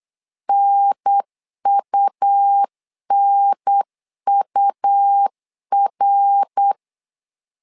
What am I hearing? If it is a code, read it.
Morse code, heard as a keyed tone, about 8 wpm, 800 Hz. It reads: NUNUR
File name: LEZIONI / 3 MWRO / 3 MWR 8wpm 2444